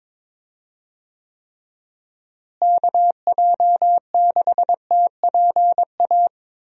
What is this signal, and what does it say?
Morse code, heard as a keyed tone, about 22 wpm, 700 Hz. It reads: KJ6TPA
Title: USA Call Signs - 6 Characters - Repeated 2x - 22wpm